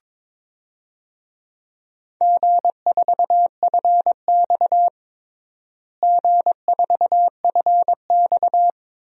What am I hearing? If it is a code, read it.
Morse code, heard as a keyed tone, about 22 wpm, 700 Hz. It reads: G4FX G4FX
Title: DXCC-100 Call Signs - 4 Characters - Repeated 2x - 22wpm